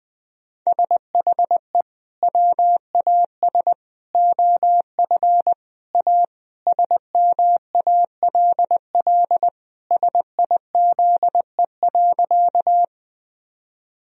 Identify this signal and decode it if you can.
Morse code, heard as a keyed tone, about 20 wpm, 700 Hz. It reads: SHE WAS OF A SMALL SIZE.
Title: Sentences from Top 1000 Words - Review Part 2 - 20wpm